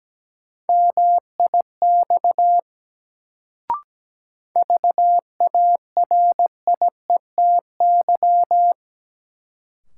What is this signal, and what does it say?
Morse code, heard as a keyed tone, about 17 wpm, 700 Hz. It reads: MIX VARIETY
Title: Sentences from Top 1000 Words - Review Part 4 - New Word Drill - 17wpm